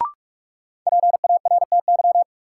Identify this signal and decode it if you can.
Morse code, heard as a keyed tone, about 46 wpm, 700 Hz. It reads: PARTY
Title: Sentences from Top 1400 Words - New Word Drill - 45wpm